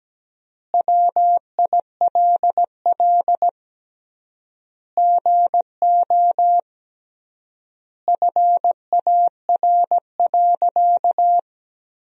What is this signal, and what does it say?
Morse code, heard as a keyed tone, about 17 wpm, 700 Hz. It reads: WILL GO FAR.
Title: 3x Word-Spacing - Sentences from Top 400 Words - 17wpm